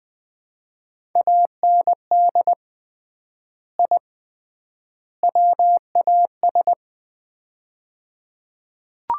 Morse code, heard as a keyed tone, about 20 wpm, 700 Hz. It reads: AND I WAS
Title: 3x Word-Spacing - Sets of 3 Words - 20wpm